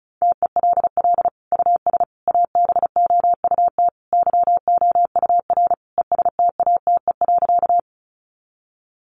Morse code, heard as a keyed tone, about 34 wpm, 700 Hz. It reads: TELL US ABOUT YOUR ESTATE.
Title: Sentences from Top 1200 Words 35wpm